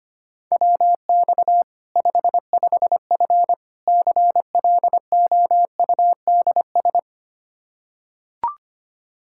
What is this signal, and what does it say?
Morse code, heard as a keyed tone, about 25 wpm, 700 Hz. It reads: WX 55F CLOUDS
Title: Practice QSO - Exchange 2 - Info - 25wpm